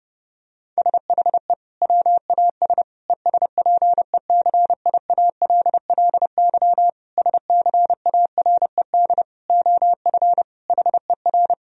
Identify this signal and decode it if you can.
Morse code, heard as a keyed tone, about 31 wpm, 700 Hz. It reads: SHE WAS ESPECIALLY SCARED OF HER
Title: Sentences from Top 1000 Words - Review Part 4 - 30wpm